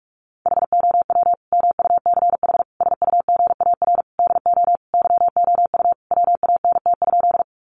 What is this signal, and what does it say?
Morse code, heard as a keyed tone, about 44 wpm, 700 Hz. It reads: HOW MUCH SUGAR DO YOU WANT?
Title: Sentences from Top 1200 Words 45wpm